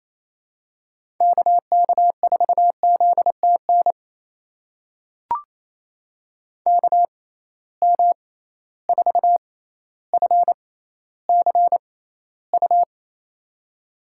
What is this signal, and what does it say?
Morse code, heard as a keyed tone, about 28 wpm, spaced as for 14 wpm, 700 Hz. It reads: KK4ZTN KM4FCU
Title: USA Call Signs - 6 Characters - 6x Character Spacing - 28wpm